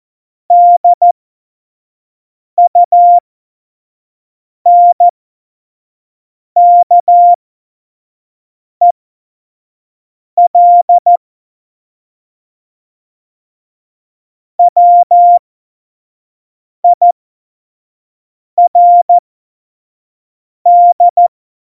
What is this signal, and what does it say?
Morse code, heard as a keyed tone, about 14 wpm, spaced as for 5 wpm, 700 Hz. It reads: DUNKEL WIRD
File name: Morsekurs / Witze / Witz 08. Oktober 2019, 25 F70